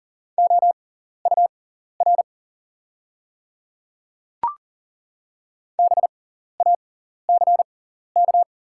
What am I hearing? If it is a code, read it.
Morse code, heard as a keyed tone, about 40 wpm, spaced as for 14 wpm, 690 Hz. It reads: OUR BACK
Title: 6x Character Spacing - Top 200 Words - 40wpm